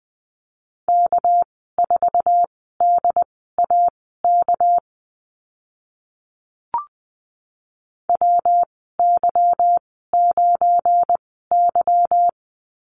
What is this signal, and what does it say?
Morse code, heard as a keyed tone, about 20 wpm, spaced as for 14 wpm, 700 Hz. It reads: K4DAK WY9Y
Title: 213 Lesson 34g - Number 8 - Call Signs 20wpm